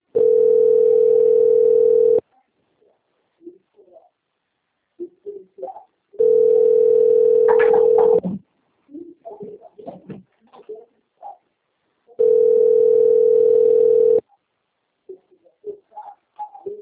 Ringback tone (North America), 440 + 480 Hz.
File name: out- - - - - .54186